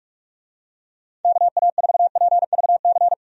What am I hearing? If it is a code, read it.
Morse code, heard as a keyed tone, about 46 wpm, 700 Hz. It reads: KA4PVC
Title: USA Call Signs - 6 Characters - 45wpm